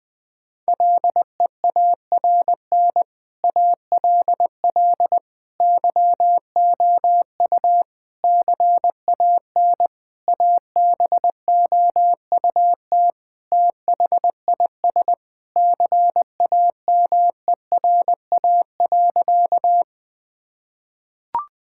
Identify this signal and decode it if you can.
Morse code, heard as a keyed tone, about 21 wpm, 700 Hz. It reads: LEARN ALL YOU CAN ABOUT THIS CAMERA.
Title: Sentences from Top 1000 Words - Review Part 5 - 20wpm